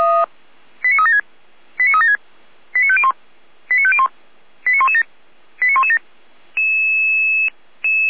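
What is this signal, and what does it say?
Five-tone selective call (ZVEI), heard as quick runs of five tones, tones of about 70 ms. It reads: 89376 89376 89501 89501 89117 89117
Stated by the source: ZVEI five-tone selective call 89092 received with an SDR.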